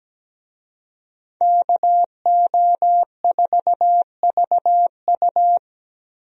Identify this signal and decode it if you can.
Morse code, heard as a keyed tone, about 17 wpm, 700 Hz. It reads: KO4VU
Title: USA Call Signs - 5 Characters - Repeated 2x - 17wpm